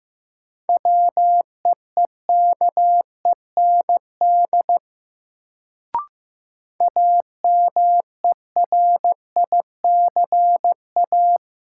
Morse code, heard as a keyed tone, about 15 wpm, 700 Hz. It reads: WEEKEND AMERICA
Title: Sets of 5 Words - New Word Drill - 15wpm